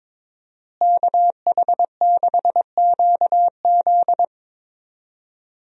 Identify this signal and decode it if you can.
Morse code, heard as a keyed tone, about 21 wpm, 700 Hz. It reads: KH6QZ
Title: USA Call Signs - 5 Characters - Repeated 2x - 22wpm